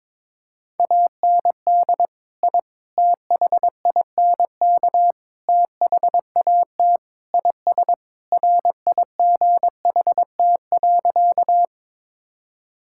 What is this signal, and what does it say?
Morse code, heard as a keyed tone, about 22 wpm, 700 Hz. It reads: AND I THINK THAT IS RIGHT.
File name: practice / 22wpm / Sentences with 4-Word Phrases - 22wpm